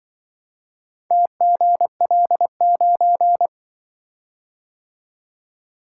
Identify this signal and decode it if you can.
Morse code, heard as a keyed tone, about 23 wpm, 700 Hz. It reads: TGL9
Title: Speed-Racing - Four Letter-Number Combinations - 20wpm